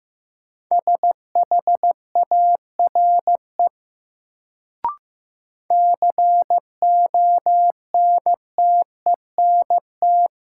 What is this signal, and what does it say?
Morse code, heard as a keyed tone, about 15 wpm, 700 Hz. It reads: SHARE CONTENT
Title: Sentences from Top 900 Words - New Word Drill - 15wpm